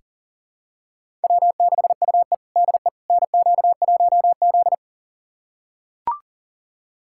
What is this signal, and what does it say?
Morse code, heard as a keyed tone, about 40 wpm, 700 Hz. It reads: W6UE DE NQ1Z
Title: USA Callsign DE Callsign - 4 Characters - 40wpm